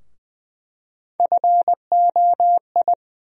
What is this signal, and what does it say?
Morse code, heard as a keyed tone, about 20 wpm, 700 Hz. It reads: FOI
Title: USA Call Sign Postfixes - 3 Letters 20wpm